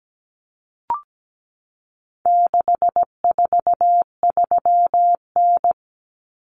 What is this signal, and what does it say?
Morse code, heard as a keyed tone, about 17 wpm, 700 Hz. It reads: 643N
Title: Common Cut Numbers - 4-Digit 17wpm